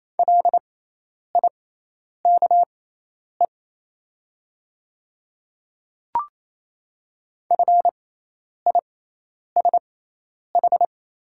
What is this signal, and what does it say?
Morse code, heard as a keyed tone, about 28 wpm, spaced as for 10 wpm, 700 Hz. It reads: LIKE FISH
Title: 6x Character Spacing - Top 300 Words - 28wpm